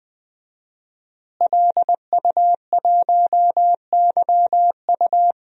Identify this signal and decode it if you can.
Morse code, heard as a keyed tone, about 20 wpm, 700 Hz. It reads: LU1YU
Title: DXCC-100 Call Signs - 5 Characters - Repeated 2x - 20wpm